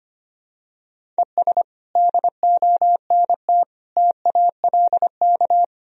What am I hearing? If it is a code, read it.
Morse code, heard as a keyed tone, about 25 wpm, 700 Hz. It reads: ES DONT TALK
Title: Sentences from Top 400 Words and Abbreviations 25wpm